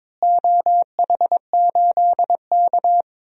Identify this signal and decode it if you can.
Morse code, heard as a keyed tone, about 22 wpm, 700 Hz. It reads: OH8K
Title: DXCC-100 Call Signs - 4 Characters - Repeated 3x - 22wpm